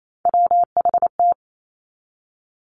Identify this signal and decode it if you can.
Morse code, heard as a keyed tone, about 28 wpm, 700 Hz. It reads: WHT
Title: Three Challenging Letter-Number Combinations 28wpm